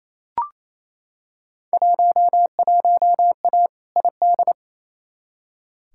Morse code, heard as a keyed tone, about 28 wpm, 700 Hz. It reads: 11A ID
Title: Field Day Exchanges Encore - 28wpm